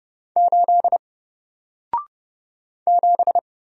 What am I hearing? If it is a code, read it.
Morse code, heard as a keyed tone, about 30 wpm, 700 Hz. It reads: 8 7
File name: Single-Letter-Number-30wpm